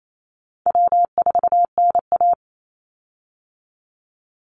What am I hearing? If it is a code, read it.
Morse code, heard as a keyed tone, about 28 wpm, 700 Hz. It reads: W4NA
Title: Easy Call Signs - 28wpm